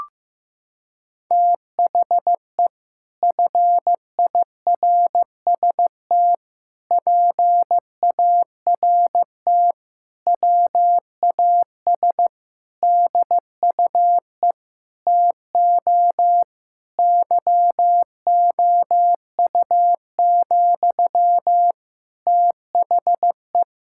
Morse code, heard as a keyed tone, about 16 wpm, 700 Hz. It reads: THE FIRST PART WAS DUE TO YOU, THE
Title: Sentences from Top 1000 Words - Review Part 6 - 15wpm